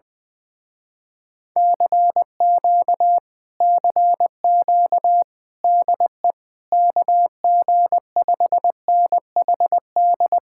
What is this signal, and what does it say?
Morse code, heard as a keyed tone, about 20 wpm, 700 Hz. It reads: CQ CQ DE KG5NHD
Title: Speed-Racing Common QSO Phrases - 20wpm